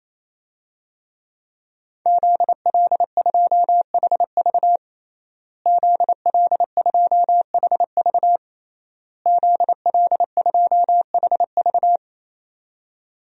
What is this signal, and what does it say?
Morse code, heard as a keyed tone, about 28 wpm, 700 Hz. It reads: ZL2HV ZL2HV ZL2HV
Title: DXCC-100 Call Signs - 5 Characters - Repeated 3x - 28wpm